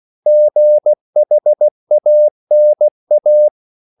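Morse code, heard as a keyed tone, about 16 wpm, 600 Hz. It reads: GHANA
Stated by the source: CW practice of national capitals at 16 words per minute, from ON6ZQ.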